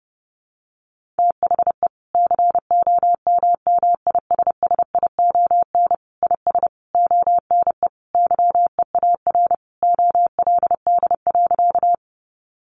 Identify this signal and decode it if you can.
Morse code, heard as a keyed tone, about 30 wpm, 700 Hz. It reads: THE COMMISSION IS ONE YEAR OLD.
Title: Sentences from Top 1400 Words 30wpm